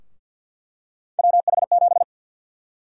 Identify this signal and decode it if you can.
Morse code, heard as a keyed tone, about 47 wpm, 700 Hz. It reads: WH7
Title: USA Call Sign Prefixes - Repeated 2x - 50wpm